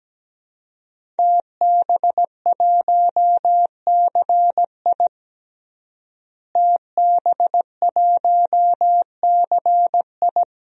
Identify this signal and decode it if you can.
Morse code, heard as a keyed tone, about 17 wpm, 700 Hz. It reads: TB1CI TB1CI
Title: DXCC-100 Call Signs - 5 Characters - Repeated 3x - 17wpm